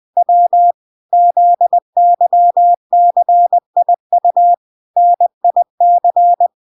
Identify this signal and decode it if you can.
Morse code, heard as a keyed tone, about 20 wpm, 700 Hz. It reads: W ZYCIU NIC